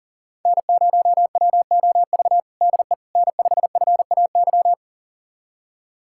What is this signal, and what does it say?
Morse code, heard as a keyed tone, about 37 wpm, 700 Hz. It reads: N0WOV DE N5FAY
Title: USA Callsign DE Callsign - 5 Characters - 40wpm